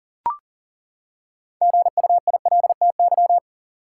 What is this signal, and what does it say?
Morse code, heard as a keyed tone, about 40 wpm, 700 Hz. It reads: GUILTY
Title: Six Letter Words - 40wpm